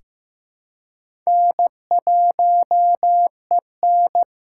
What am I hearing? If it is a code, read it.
Morse code, heard as a keyed tone, about 16 wpm, 700 Hz. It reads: N1EN
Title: K1USN SST QSO Practice - 15wpm